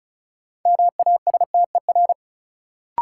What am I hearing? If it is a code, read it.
Morse code, heard as a keyed tone, about 36 wpm, 700 Hz. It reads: MASTER
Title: Sentences from Top 1300 Words - New Word Drill - 35wpm